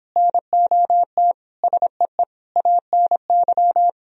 Morse code, heard as a keyed tone, about 29 wpm, 700 Hz. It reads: NOT SEE ANY